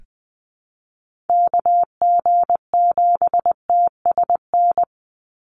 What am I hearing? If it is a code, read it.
Morse code, heard as a keyed tone, about 20 wpm, 700 Hz. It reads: KG7TSN
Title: USA Call Signs 20wpm